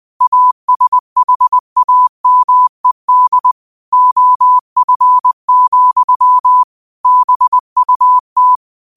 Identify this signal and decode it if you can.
Morse code, heard as a keyed tone, about 20 wpm, 1000 Hz. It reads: ASHAMED OF, BUT
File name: 2025-06-25-20wpm